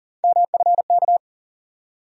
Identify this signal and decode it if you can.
Morse code, heard as a keyed tone, about 39 wpm, 700 Hz. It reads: MFK
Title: USA Call Sign Postfixes - 3 Letters 40wpm